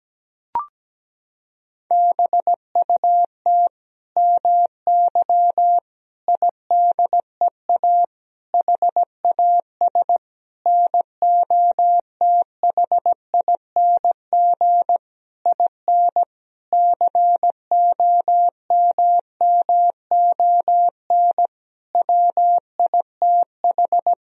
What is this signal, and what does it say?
Morse code, heard as a keyed tone, about 17 wpm, 700 Hz. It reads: BUT MY IDEA HAS NOTHING IN COMMON WITH
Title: Sentences from Top 1000 Words - Review Part 3 - 17wpm